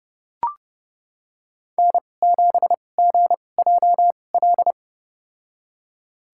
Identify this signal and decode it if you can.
Morse code, heard as a keyed tone, about 29 wpm, spaced as for 22 wpm, 700 Hz. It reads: N7GJL
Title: USA Call Signs - 5 Characters - 2x Character Spacing - 30wpm